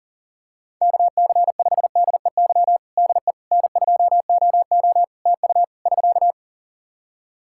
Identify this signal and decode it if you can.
Morse code, heard as a keyed tone, about 40 wpm, 700 Hz. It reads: KC5DEY DE N2OO TU <SK>
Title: K1USN SST QSO Practice 40wpm